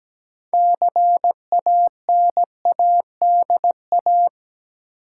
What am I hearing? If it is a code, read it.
Morse code, heard as a keyed tone, about 17 wpm, 700 Hz. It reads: CANADA
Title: Sets of 5 Words - New Word Drill - 17wpm